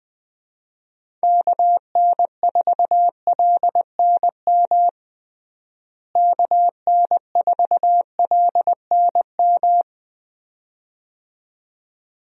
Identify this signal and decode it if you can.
Morse code, heard as a keyed tone, about 20 wpm, 700 Hz. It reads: KN4LNM KN4LNM
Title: USA Call Signs - 6 Characters - Repeated 2x - 20wpm